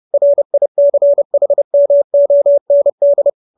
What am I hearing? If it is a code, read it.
Morse code, heard as a keyed tone, about 30 wpm, 570 Hz. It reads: RICHMOND